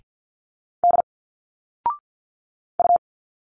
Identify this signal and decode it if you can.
Morse code, heard as a keyed tone, about 50 wpm, 700 Hz. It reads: D U
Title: Single Letters 50wpm